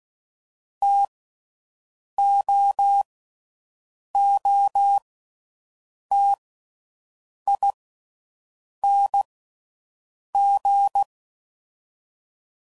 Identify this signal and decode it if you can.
Morse code, heard as a keyed tone, about 16 wpm, 800 Hz. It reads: T O O T I N G
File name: Combinations / Difficult Combos / T AOEI's